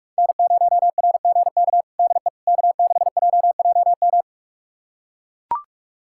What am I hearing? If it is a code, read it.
Morse code, heard as a keyed tone, about 41 wpm, 700 Hz. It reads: N0RGK DE K6JJM
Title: USA Callsign DE Callsign - 5 Characters - 45wpm